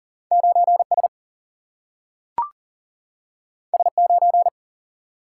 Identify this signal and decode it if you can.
Morse code, heard as a keyed tone, about 40 wpm, 700 Hz. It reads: 9S S9
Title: Single-Letter Single-Number 40wpm